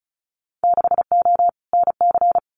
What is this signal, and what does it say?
Morse code, heard as a keyed tone, about 34 wpm, 700 Hz. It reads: 6O NC
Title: Winter Field Day Exchanges 35wpm